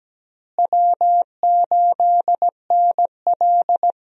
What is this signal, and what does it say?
Morse code, heard as a keyed tone, about 17 wpm, 700 Hz. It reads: W8NL
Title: USA Call Signs - 4 Characters - Repeated 3x - 17wpm